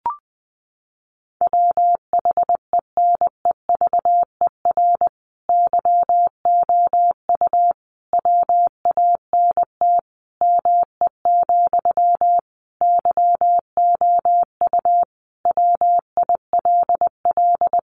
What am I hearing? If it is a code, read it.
Morse code, heard as a keyed tone, about 20 wpm, 700 Hz. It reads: WHENEVER YOU WANT ME, YOU WILL
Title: Sentences from Top 1200 Words 20wpm